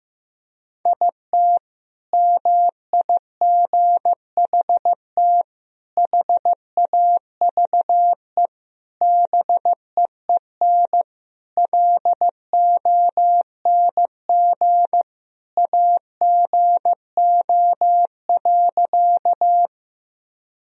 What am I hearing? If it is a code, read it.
Morse code, heard as a keyed tone, about 15 wpm, 700 Hz. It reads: IT MIGHT HAVE BEEN LONG AGO.
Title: Sentences from Top 1000 Words - Review Part 3 - 15wpm